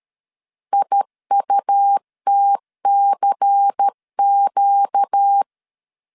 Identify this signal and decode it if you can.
Morse code, heard as a keyed tone, about 13 wpm, 800 Hz. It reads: IUTCQ
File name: CW / DET / 1 9 1500C S B 12wpm 3819